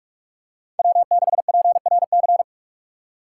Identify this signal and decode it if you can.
Morse code, heard as a keyed tone, about 44 wpm, 700 Hz. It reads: W6PRC
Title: USA Call Signs - 5 Characters - 3x Character Spacing - 45wpm